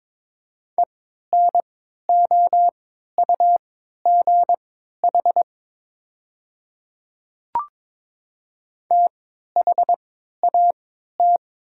Morse code, heard as a keyed tone, about 22 wpm, spaced as for 12 wpm, 700 Hz. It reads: ENOUGH THAT
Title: 3x Character Spacing - Top 500 Words - 22wpm